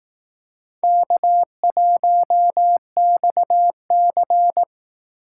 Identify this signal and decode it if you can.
Morse code, heard as a keyed tone, about 18 wpm, 700 Hz. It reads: K1XC
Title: Speed Racing - USA 1x2 Call Signs 15wpm